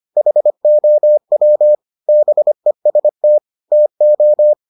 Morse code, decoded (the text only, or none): HOW BEST TO